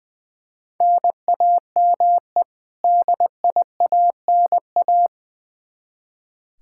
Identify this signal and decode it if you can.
Morse code, heard as a keyed tone, about 20 wpm, 700 Hz. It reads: NAME DIANA
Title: Practice QSO - Exchange 1 - Beginning - 20wpm